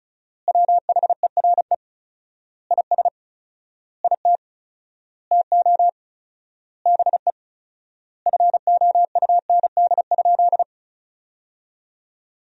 Morse code, heard as a keyed tone, about 36 wpm, 700 Hz. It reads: WHERE IS IT TO BE FOUND?
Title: 4x Word-Spacing - Sentences from Top 300 Words - 35wpm